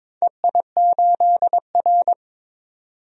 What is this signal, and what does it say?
Morse code, heard as a keyed tone, about 21 wpm, 700 Hz. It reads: EI8R
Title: DXCC-100 Call Signs - 4 Characters - 22wpm